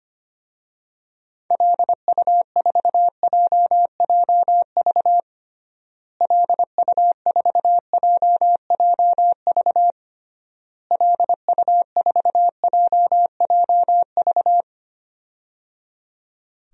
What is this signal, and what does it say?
Morse code, heard as a keyed tone, about 25 wpm, 700 Hz. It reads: LU4JJV LU4JJV LU4JJV
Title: DXCC-100 Call Signs - 6 Characters - Repeated 3x - 25wpm